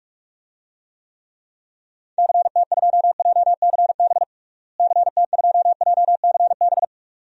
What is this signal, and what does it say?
Morse code, heard as a keyed tone, about 43 wpm, 700 Hz. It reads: CT2JCB CT2JCB
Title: DXCC-100 Call Signs - 6 Characters - Repeated 2x - 45wpm